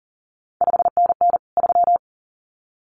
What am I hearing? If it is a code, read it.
Morse code, heard as a keyed tone, about 41 wpm, 700 Hz. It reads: 5NN 3
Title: CQ WW DX Contest Exchange Encore - 40wpm